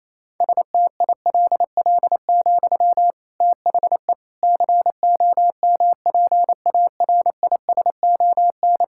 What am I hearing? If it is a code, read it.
Morse code, heard as a keyed tone, about 28 wpm, 700 Hz. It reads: STILL, THE COMPARISON
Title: Sentences from Top 1600 Words 28wpm